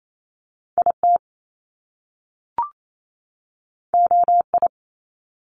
Morse code, heard as a keyed tone, about 28 wpm, 700 Hz. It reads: IT OI
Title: Top Two Letter Combinations in Words 28wpm